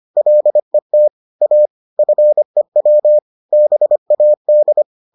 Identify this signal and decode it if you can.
Morse code, heard as a keyed tone, about 25 wpm, 600 Hz. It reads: LET A FEW BAD